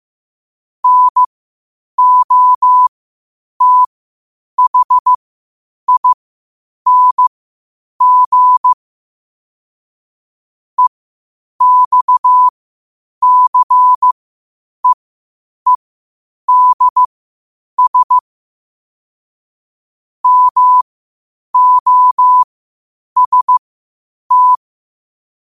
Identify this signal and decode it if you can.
Morse code, heard as a keyed tone, about 15 wpm, spaced as for 8 wpm, 1000 Hz. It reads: NOTHING EXCEEDS MOST
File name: qotdfiles / 8wpm